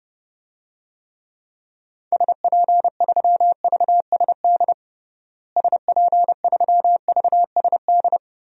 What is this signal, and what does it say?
Morse code, heard as a keyed tone, about 30 wpm, 700 Hz. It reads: SP3VSD SP3VSD
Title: DXCC-100 Call Signs - 6 Characters - Repeated 3x - 30wpm